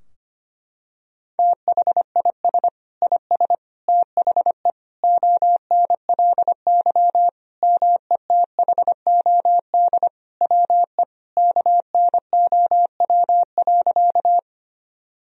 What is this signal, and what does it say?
Morse code, heard as a keyed tone, about 25 wpm, 700 Hz. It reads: THIS IS THE ONLY METHOD WE KNOW.
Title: Sentences from Top 1000 Words - Review Part 3 - 25wpm